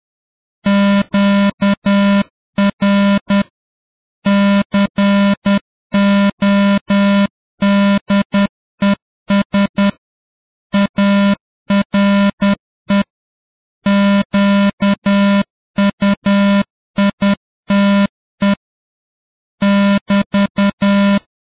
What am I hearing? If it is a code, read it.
Morse code, heard as a keyed tone, about 10 wpm, 200 Hz. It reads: QR CODES ARE QUITE <BT>